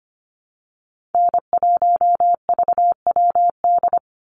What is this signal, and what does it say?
Morse code, heard as a keyed tone, about 25 wpm, 700 Hz. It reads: N1VWD
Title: NA SOTA Chaser Callsigns 25wpm